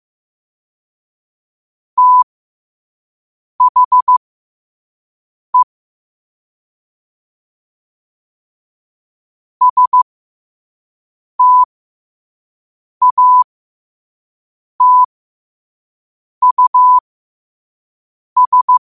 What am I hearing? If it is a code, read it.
Morse code, heard as a keyed tone, about 15 wpm, spaced as for 5 wpm, 1000 Hz. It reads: THE STATUS
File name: qotdfiles / 5wpm